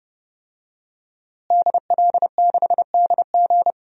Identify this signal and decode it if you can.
Morse code, heard as a keyed tone, about 30 wpm, 700 Hz. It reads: DL6DG
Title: DXCC-100 Call Signs - 5 Characters - Repeated 3x - 30wpm